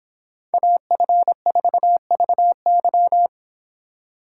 Morse code, heard as a keyed tone, about 26 wpm, 700 Hz. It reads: AF4VY